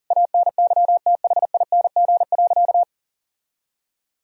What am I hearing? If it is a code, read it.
Morse code, heard as a keyed tone, about 40 wpm, 700 Hz. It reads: ANYTHING.
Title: Sentences from Top 1000 Words - Review Part 2 - 40wpm